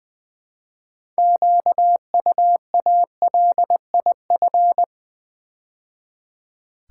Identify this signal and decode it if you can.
Morse code, heard as a keyed tone, about 20 wpm, 700 Hz. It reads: QUALIF
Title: Build A Word - 3 Letters At A Time - 9-Letter Words - 20wpm